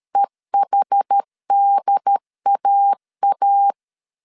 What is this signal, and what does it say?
Morse code, heard as a keyed tone, about 13 wpm, 800 Hz. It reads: EHDAA
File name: CW / DET / 1 3 1500C S C 12wpm 2750